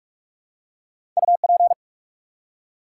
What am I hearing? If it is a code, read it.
Morse code, heard as a keyed tone, about 44 wpm, 700 Hz. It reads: UP
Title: Two Letter Words - 45wpm